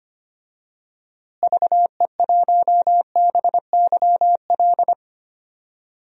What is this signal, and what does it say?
Morse code, heard as a keyed tone, about 25 wpm, 700 Hz. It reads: VE1BYL
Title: DXCC-100 Call Signs - 6 Characters - Repeated 3x - 25wpm